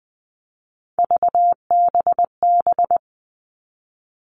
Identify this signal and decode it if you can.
Morse code, heard as a keyed tone, about 20 wpm, 700 Hz. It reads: VBB
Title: Three Challenging Letter-Number Combinations 20wpm